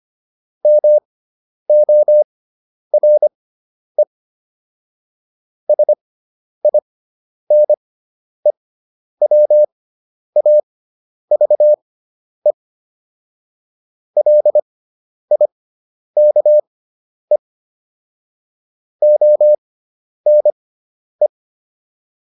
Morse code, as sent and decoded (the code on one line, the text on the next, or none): -- --- .-. / ... .. -. .-- .- ...- . / .-.. .. -.- . / --- -. .
MORE SINEWAVE LIKE ONE